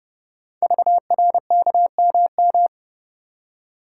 Morse code, heard as a keyed tone, about 29 wpm, 690 Hz. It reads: VRKMM
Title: Code Groups - 5 Letters - 30wpm